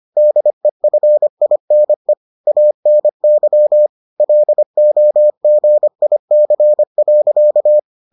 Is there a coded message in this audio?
Yes: Morse code from 0.2 to 7.8 s, heard as a keyed tone, about 25 wpm, 600 Hz. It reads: DEFINE ANY LOGIC.